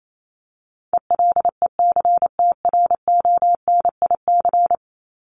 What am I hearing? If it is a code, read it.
Morse code, heard as a keyed tone, about 28 wpm, 700 Hz. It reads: ELECTRONIC
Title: Top 1001-1250 Words 28wpm